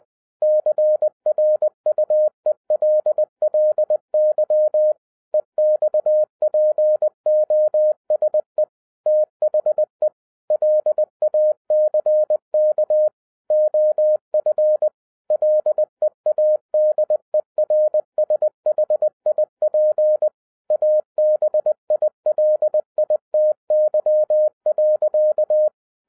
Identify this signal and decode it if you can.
Morse code, heard as a keyed tone, about 20 wpm, 610 Hz. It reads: CRUELLY EXPOSE THE LACK OF LEADERSHIP ABILITY.